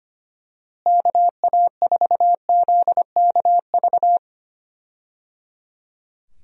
Morse code, heard as a keyed tone, about 25 wpm, 700 Hz. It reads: KA4ZKV
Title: USA Call Signs - 6 Characters - Repeated 3x - 25wpm